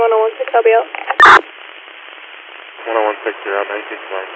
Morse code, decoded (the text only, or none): none